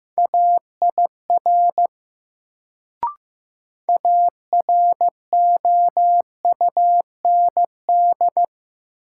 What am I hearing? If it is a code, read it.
Morse code, heard as a keyed tone, about 15 wpm, 700 Hz. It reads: AIR AROUND
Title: Sets of 2 Words - New Word Drill - 15wpm